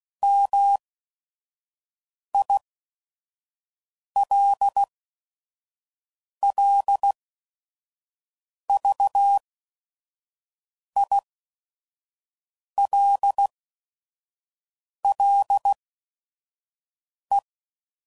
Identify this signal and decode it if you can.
Morse code, heard as a keyed tone, about 16 wpm, spaced as for 5 wpm, 800 Hz. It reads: MILLVILLE
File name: CW Code Groups @5WPM / fromSM's / Cities